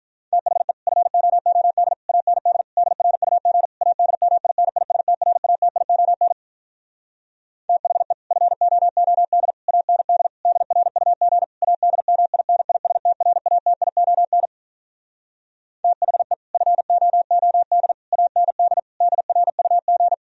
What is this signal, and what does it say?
Morse code, heard as a keyed tone, about 47 wpm, 700 Hz. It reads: THE FOOD AND DRUG ADMINISTRATION THE FOOD AND DRUG ADMINISTRATION THE FOOD AND DRUG